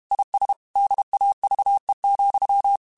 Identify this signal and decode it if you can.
Morse code, heard as a keyed tone, about 32 wpm, 800 Hz. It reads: IS DAVE,
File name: QSOs / QSOs-mp3s / kc7haa@30